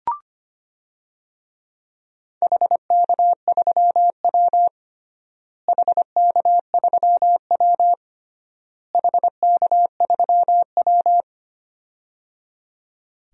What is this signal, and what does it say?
Morse code, heard as a keyed tone, about 25 wpm, 700 Hz. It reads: HK3W HK3W HK3W
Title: DXCC-100 Call Signs - 4 Characters - Repeated 3x - 25wpm